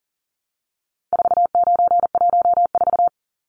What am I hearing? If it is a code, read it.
Morse code, heard as a keyed tone, about 40 wpm, 700 Hz. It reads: 4914
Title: Numbers - 4-Digit 40wpm